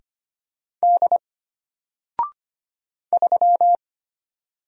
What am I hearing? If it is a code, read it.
Morse code, heard as a keyed tone, about 25 wpm, 700 Hz. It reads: D 3
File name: Single-Letter-Number-25wpm